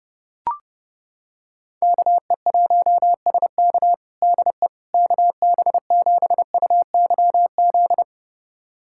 Morse code, heard as a keyed tone, about 30 wpm, 700 Hz. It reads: KE1SK DE KB7UYZ
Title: USA Callsign DE Callsign 30wpm